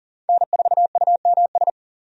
Morse code, heard as a keyed tone, about 40 wpm, 700 Hz. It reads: N4UMS
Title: USA Call Signs - 5 Characters - 40wpm